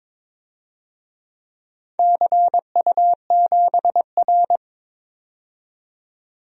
Morse code, heard as a keyed tone, about 21 wpm, 700 Hz. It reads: CU7R
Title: DXCC-100 Call Signs - 4 Characters - 22wpm